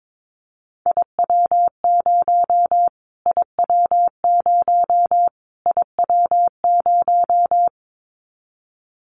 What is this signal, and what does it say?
Morse code, heard as a keyed tone, about 22 wpm, 700 Hz. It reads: IW0 IW0 IW0
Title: Callsign Prefixes for Top 25 DXCC - Repeated 3x - 22wpm